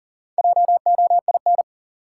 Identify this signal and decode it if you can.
Morse code, heard as a keyed tone, about 40 wpm, 700 Hz. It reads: JOIN
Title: Sentences from Top 1000 Words - Review Part 6 - New Word Drill - 40wpm